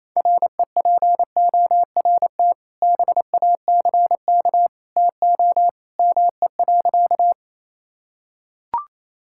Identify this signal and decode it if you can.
Morse code, heard as a keyed tone, about 28 wpm, 700 Hz. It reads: REPORT BACK TO ME.
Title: Sentences from Top 1000 Words - Review Part 6 - 28wpm